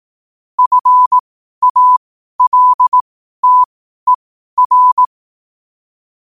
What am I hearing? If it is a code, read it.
Morse code, heard as a keyed tone, about 18 wpm, spaced as for 12 wpm, 1000 Hz. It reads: FALTER
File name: qotdfiles / 12wpm